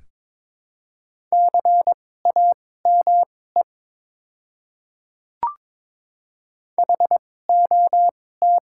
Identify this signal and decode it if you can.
Morse code, heard as a keyed tone, about 21 wpm, spaced as for 16 wpm, 700 Hz. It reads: CAME HOT
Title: 2x Character Spacing - Top 300 Words - 22wpm